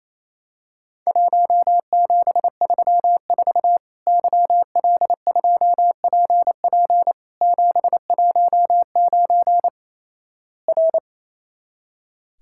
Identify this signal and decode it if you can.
Morse code, heard as a keyed tone, about 28 wpm, 700 Hz. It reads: 1734 YL2PP 719 R
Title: Worked All Europe DX Contest - QTC Exchange - 28wpm